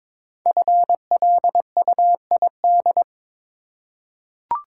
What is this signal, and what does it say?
Morse code, heard as a keyed tone, about 23 wpm, 700 Hz. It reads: FLUID
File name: 5 Letter Words - First 3 Letters Given - 22wpm